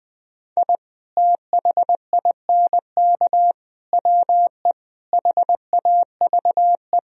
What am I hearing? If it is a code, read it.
Morse code, heard as a keyed tone, about 20 wpm, 700 Hz. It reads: I THINK WE HAVE
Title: Speed-Racing - Sets of 5 Words 17wpm